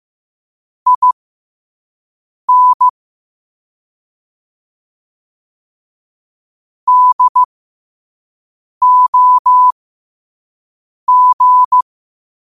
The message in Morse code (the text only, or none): IN DOG